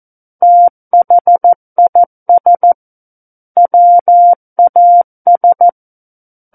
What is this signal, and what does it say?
Morse code, heard as a keyed tone, about 14 wpm, 700 Hz. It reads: THIS WAS
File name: Book One / 05-Chapter 5 - 14 WPM